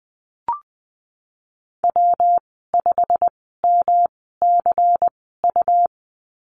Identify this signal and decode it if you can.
Morse code, heard as a keyed tone, about 20 wpm, spaced as for 15 wpm, 700 Hz. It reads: W5MCU
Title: 167 Lesson 27g - Number 7 - Call Signs 20wpm